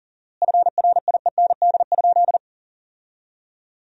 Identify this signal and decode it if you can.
Morse code, heard as a keyed tone, about 40 wpm, 700 Hz. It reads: FRIEND?
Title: 4x Word-Spacing - Sentences from Top 300 Words - 40wpm